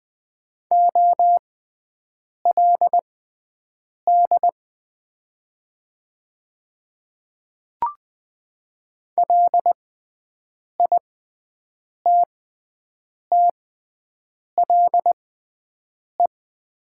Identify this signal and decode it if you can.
Morse code, heard as a keyed tone, about 20 wpm, spaced as for 7 wpm, 700 Hz. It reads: OLD LITTLE
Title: 6x Character Spacing - Top 200 Words - 20wpm